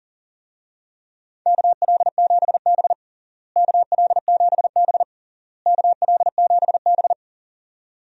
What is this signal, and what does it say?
Morse code, heard as a keyed tone, about 40 wpm, 700 Hz. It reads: KL7B KL7B KL7B
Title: DXCC-100 Call Signs - 4 Characters - Repeated 3x - 40wpm